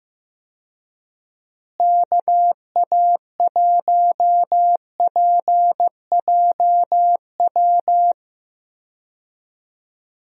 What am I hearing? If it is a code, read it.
Morse code, heard as a keyed tone, about 16 wpm, 700 Hz. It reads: KA1PJW
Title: USA Call Signs - 6 Characters - Repeated 3x - 15wpm